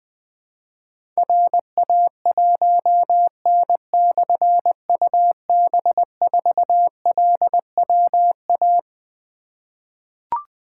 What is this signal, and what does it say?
Morse code, heard as a keyed tone, about 21 wpm, 700 Hz. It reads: RA1N/UB4LWA
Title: Difficult Call Signs 20wpm